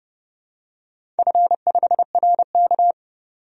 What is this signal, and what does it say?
Morse code, heard as a keyed tone, about 30 wpm, 700 Hz. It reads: F5RK
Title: DXCC-100 Call Signs - 4 Characters - 30wpm